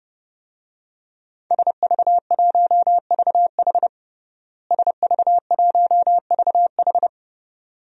Morse code, heard as a keyed tone, about 30 wpm, 700 Hz. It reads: SV1VH SV1VH
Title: DXCC-100 Call Signs - 5 Characters - Repeated 2x - 30wpm